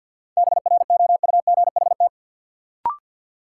Morse code, heard as a keyed tone, about 53 wpm, 700 Hz. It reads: BROUGHT
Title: Sentences from Top 500 Words - New Word Drill - 50wpm